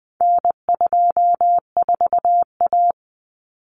Morse code, heard as a keyed tone, about 20 wpm, 700 Hz. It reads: N24A
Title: Common Cut Numbers - 4-Digit 20wpm